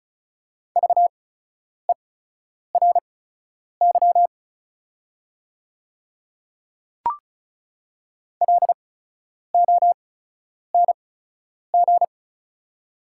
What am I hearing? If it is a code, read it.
Morse code, heard as a keyed tone, about 34 wpm, spaced as for 10 wpm, 690 Hz. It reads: VERY LONG
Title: 8x Character Spacing - Top 300 Words - 35wpm